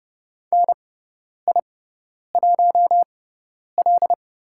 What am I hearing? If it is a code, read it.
Morse code, heard as a keyed tone, about 30 wpm, spaced as for 10 wpm, 700 Hz. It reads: NI1L
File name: Lesson 16g - Letter c - Call Signs 30wpm